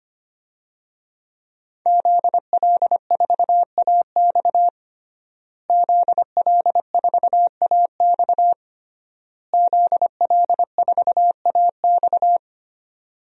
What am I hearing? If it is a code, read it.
Morse code, heard as a keyed tone, about 25 wpm, 700 Hz. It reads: ZL4AX ZL4AX ZL4AX
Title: DXCC-100 Call Signs - 5 Characters - Repeated 3x - 25wpm